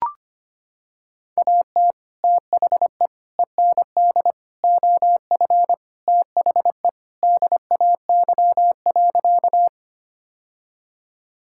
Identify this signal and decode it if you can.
Morse code, heard as a keyed tone, about 25 wpm, 700 Hz. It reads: AT THE END OF THE DAY.
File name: practice / 25wpm / Sentences with 2-Word Phrases - 25wpm